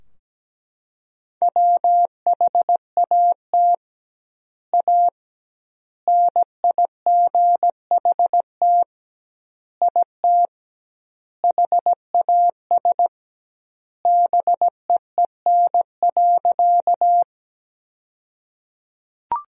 Morse code, heard as a keyed tone, about 17 wpm, 700 Hz. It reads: WHAT A NIGHT IT HAS BEEN.
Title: 2x Word-Spacing - Sentences from Top 300 Words - 17wpm